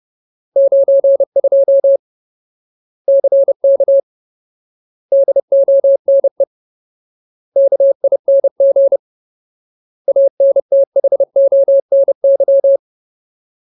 Morse code, heard as a keyed tone, about 30 wpm, 550 Hz. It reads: 92 CK DONE KING ANTHONY
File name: Ultimate Word List / 30WPM Ultimate